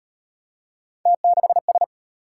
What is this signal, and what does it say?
Morse code, heard as a keyed tone, about 36 wpm, 700 Hz. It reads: T6S